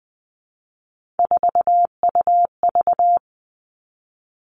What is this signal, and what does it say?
Morse code, heard as a keyed tone, about 20 wpm, 700 Hz. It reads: 4UV
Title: All Cut Numbers - 3-Digit 20wpm